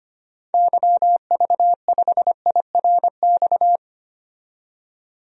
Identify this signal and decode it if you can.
Morse code, heard as a keyed tone, about 25 wpm, 700 Hz. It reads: YV5IRX
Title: DXCC-100 Call Signs - 6 Characters - 25wpm